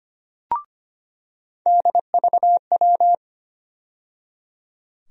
Morse code, heard as a keyed tone, about 25 wpm, 700 Hz. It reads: DVW
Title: USA Call Sign Postfixes - 3 Letters 25wpm